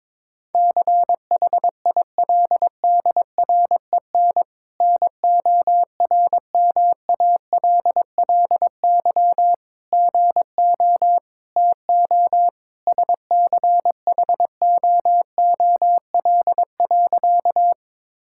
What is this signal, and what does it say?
Morse code, heard as a keyed tone, about 22 wpm, 700 Hz. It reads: CHILDREN NORMALLY GO TO SCHOOL.
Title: Sentences from Top 1000 Words - Review Part 6 - 22wpm